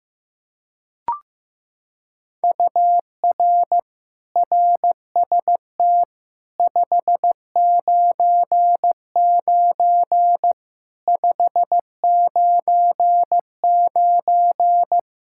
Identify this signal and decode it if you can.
Morse code, heard as a keyed tone, about 15 wpm, 700 Hz. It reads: UR RST 599 599
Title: Practice QSO - Exchange 1 - Beginning - 15wpm